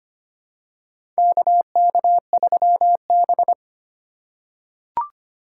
Morse code, heard as a keyed tone, about 25 wpm, 700 Hz. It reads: KK3B
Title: USA Call Signs - 4 Characters - 3x Character Spacing - 25wpm